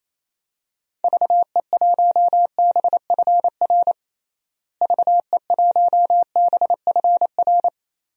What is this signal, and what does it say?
Morse code, heard as a keyed tone, about 28 wpm, 700 Hz. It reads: VE1BFR VE1BFR
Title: DXCC-100 Call Signs - 6 Characters - Repeated 2x - 28wpm